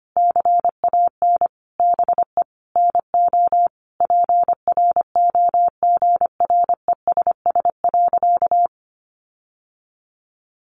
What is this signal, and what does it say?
Morse code, heard as a keyed tone, about 25 wpm, 700 Hz. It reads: CAN BE NO PROGRESS.
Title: Sentences from Top 1400 Words 25wpm